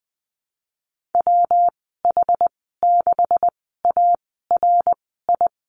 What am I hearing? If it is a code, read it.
Morse code, heard as a keyed tone, about 20 wpm, spaced as for 14 wpm, 700 Hz. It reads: WH6ARI
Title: 140 Lesson 23g - Letter f - Call Signs 20wpm